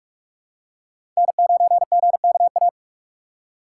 Morse code, heard as a keyed tone, about 43 wpm, 700 Hz. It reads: N9GKA
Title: USA Call Signs - 5 Characters - 2x Character Spacing - 45wpm